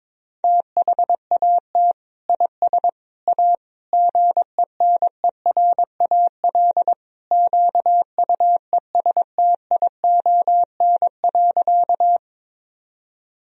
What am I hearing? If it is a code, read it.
Morse code, heard as a keyed tone, about 22 wpm, 700 Hz. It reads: THAT IS A GENERAL QUESTION.
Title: Sentences from Top 1000 Words - Review Part 3 - 22wpm